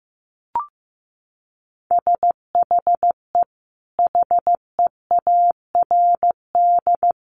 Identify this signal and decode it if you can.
Morse code, heard as a keyed tone, about 15 wpm, 700 Hz. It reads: SHE HEARD
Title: Sentences from Top 1300 Words 15wpm